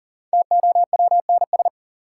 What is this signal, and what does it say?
Morse code, heard as a keyed tone, about 38 wpm, 700 Hz. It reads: TOWNS
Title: Sentences from Top 1000 Words - Review Part 6 - New Word Drill - 40wpm